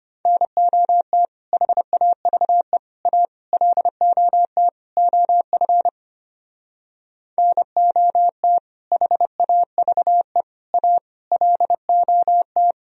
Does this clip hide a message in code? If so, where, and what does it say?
Yes: Morse code from 0.3 to 12.7 s, heard as a keyed tone, about 32 wpm, 700 Hz. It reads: NOT HAVE A LOT OF NOT HAVE A LOT